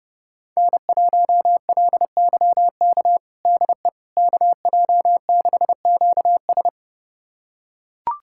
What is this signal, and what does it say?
Morse code, heard as a keyed tone, about 32 wpm, 700 Hz. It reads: N1LYK DE KJ6QS
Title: USA Callsign DE Callsign - 5 Characters - 30wpm